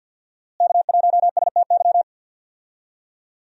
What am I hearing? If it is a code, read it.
Morse code, heard as a keyed tone, about 47 wpm, 700 Hz. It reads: K1STY
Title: USA Call Signs - 5 Characters - Repeated 3x - 50wpm